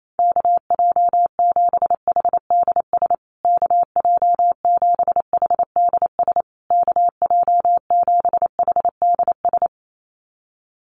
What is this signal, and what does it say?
Morse code, heard as a keyed tone, about 28 wpm, 700 Hz. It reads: KJ7HDS KJ7HDS KJ7HDS
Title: USA Call Signs Repeated 3x 28wpm